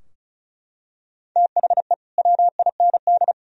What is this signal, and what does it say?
Morse code, heard as a keyed tone, about 36 wpm, 700 Hz. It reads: THE WIND
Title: Sentences from Top 1000 Words - Review Part 2 - 35wpm